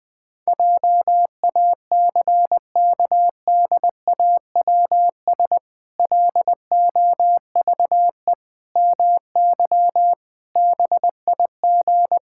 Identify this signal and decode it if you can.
Morse code, heard as a keyed tone, about 20 wpm, 700 Hz. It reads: JACKDAWS LOVE MY BIG
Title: Pangram Challenge v2 - 20wpm